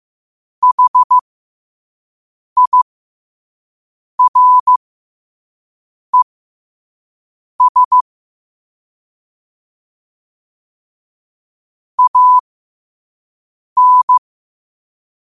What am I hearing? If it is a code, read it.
Morse code, heard as a keyed tone, about 15 wpm, spaced as for 5 wpm, 1000 Hz. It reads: HIRES AN
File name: qotdfiles / 5wpm